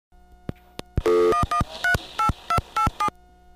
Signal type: DTMF